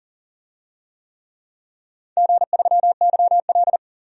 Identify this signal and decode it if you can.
Morse code, heard as a keyed tone, about 37 wpm, 700 Hz. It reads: G3YL
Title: DXCC-100 Call Signs - 4 Characters - 40wpm